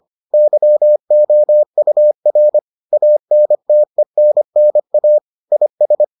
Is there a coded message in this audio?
Yes: Morse code from 0.3 to 6.0 s, heard as a keyed tone, about 25 wpm, 600 Hz. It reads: YOUR ANTENNA IS